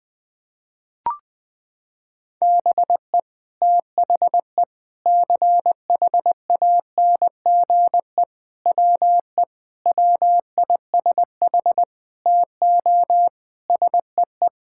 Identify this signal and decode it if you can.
Morse code, heard as a keyed tone, about 20 wpm, 700 Hz. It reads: BE THE CHANGE WE WISH TO SEE
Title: Sets of 8 Words - 20wpm